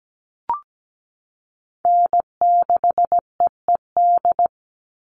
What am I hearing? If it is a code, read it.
Morse code, heard as a keyed tone, about 17 wpm, 700 Hz. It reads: N6EED
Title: Call Signs - 17wpm